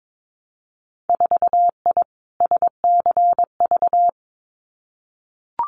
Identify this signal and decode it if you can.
Morse code, heard as a keyed tone, about 22 wpm, 700 Hz. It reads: 4I SCV
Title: Winter Field Day Exchanges Encore 22wpm